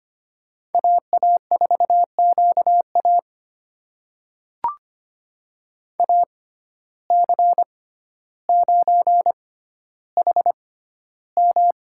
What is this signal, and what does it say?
Morse code, heard as a keyed tone, about 25 wpm, spaced as for 13 wpm, 700 Hz. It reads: AA4QA AC9HM
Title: USA Call Signs - 5 Characters - 6x Character Spacing - 25wpm